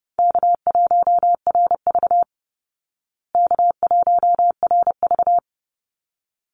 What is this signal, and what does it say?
Morse code, heard as a keyed tone, about 28 wpm, 700 Hz. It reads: K1RV K1RV
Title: Easy Call Signs Repeated 2x - 30wpm